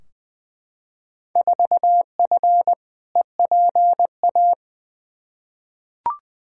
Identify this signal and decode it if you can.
Morse code, heard as a keyed tone, about 21 wpm, 700 Hz. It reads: 4F EPA
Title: Field Day Exchanges - 20wpm